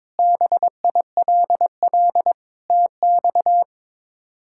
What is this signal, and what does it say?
Morse code, heard as a keyed tone, about 22 wpm, 700 Hz. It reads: BILL TX
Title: K1USN SST QSO Practice - 22wpm